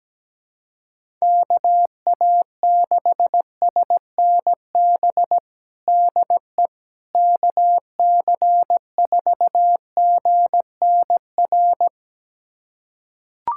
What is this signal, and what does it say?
Morse code, heard as a keyed tone, about 17 wpm, 700 Hz. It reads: KA6SNB DE KC4GNR
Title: USA Callsign DE Callsign - 6 Characters - 17wpm